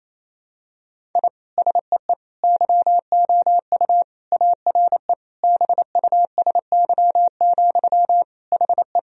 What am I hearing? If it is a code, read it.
Morse code, heard as a keyed tone, about 28 wpm, 700 Hz. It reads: I SEE YOU ARE BUSY, HE
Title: Sentences from Top 1000 Words - Review Part 3 - 28wpm